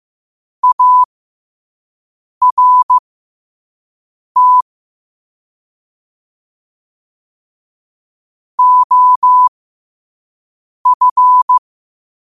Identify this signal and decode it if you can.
Morse code, heard as a keyed tone, about 15 wpm, spaced as for 5 wpm, 1000 Hz. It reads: ART OF
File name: qotdfiles / 5wpm